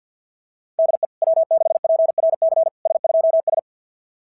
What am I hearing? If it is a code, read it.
Morse code, heard as a keyed tone, about 52 wpm, 660 Hz. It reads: DE W6PRC S2S
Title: SOTA QSO Practice 50wpm